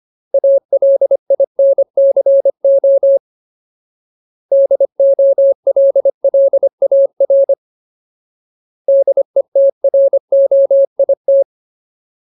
Morse code, heard as a keyed tone, about 25 wpm, 550 Hz. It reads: ALINCO DOLLAR DETROIT
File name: Ultimate Word List / 25WPM Ultimate